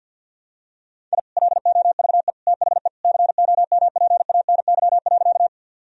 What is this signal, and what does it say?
Morse code, heard as a keyed tone, about 47 wpm, 700 Hz. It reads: I LOVE THE COMPANY.
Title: Sentences from Top 1000 Words - Review Part 3 - 50wpm